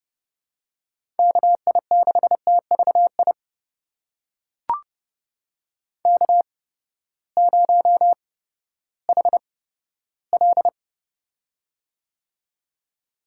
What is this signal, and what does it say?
Morse code, heard as a keyed tone, about 30 wpm, spaced as for 15 wpm, 700 Hz. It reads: KI6TVI K0HL
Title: USA Call Signs - 8x Character Spacing - 30wpm